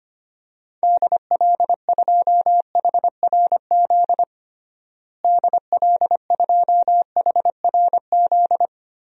Morse code, heard as a keyed tone, about 25 wpm, 700 Hz. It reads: DL2HRZ DL2HRZ
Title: DXCC-100 Call Signs - 6 Characters - Repeated 3x - 25wpm